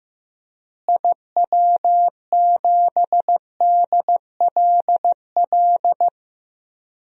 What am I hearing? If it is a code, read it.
Morse code, heard as a keyed tone, about 15 wpm, 700 Hz. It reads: IW7DLL
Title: DXCC-100 Call Signs - 6 Characters - Repeated 3x - 15wpm